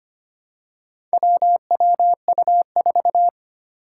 Morse code, heard as a keyed tone, about 25 wpm, 700 Hz. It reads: WWU4
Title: Four Challenging Letter-Number Combinations 25wpm